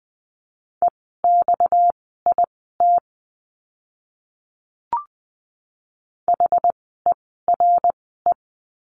Morse code, heard as a keyed tone, about 20 wpm, spaced as for 14 wpm, 700 Hz. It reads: EXIT HERE
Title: 230 Lesson 37f - Letter x - Words 20wpm